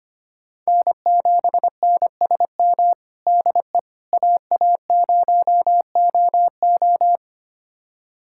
Morse code, heard as a keyed tone, about 25 wpm, 700 Hz. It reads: N7NSM DE AA0OO
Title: USA Callsign DE Callsign - 5 Characters - 25wpm